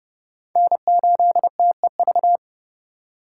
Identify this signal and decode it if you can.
Morse code, heard as a keyed tone, about 30 wpm, 700 Hz. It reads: N8TEV
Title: USA Call Signs - 5 Characters - 4x Character Spacing - 30wpm